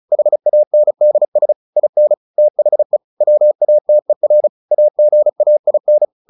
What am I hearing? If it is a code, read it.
Morse code, heard as a keyed tone, about 34 wpm, 600 Hz. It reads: HANDS IN THE WATER AGAIN
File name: the old man and the sea 2 35wpm